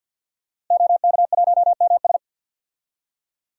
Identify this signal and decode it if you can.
Morse code, heard as a keyed tone, about 48 wpm, 700 Hz. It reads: OK1MS
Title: DXCC-100 Call Signs - 5 Characters - Repeated 2x - 50wpm